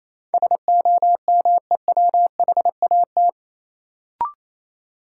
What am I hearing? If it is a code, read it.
Morse code, heard as a keyed tone, about 28 wpm, 700 Hz. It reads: SOMEWHAT
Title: Sentences from Top 1000 Words - Review Part 1 - New Word Drill - 28wpm